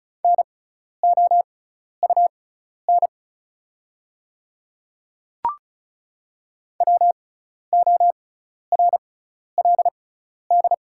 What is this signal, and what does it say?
Morse code, heard as a keyed tone, about 36 wpm, spaced as for 12 wpm, 690 Hz. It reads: NOUN WORLD
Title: 6x Character Spacing - Top 500 Words - 35wpm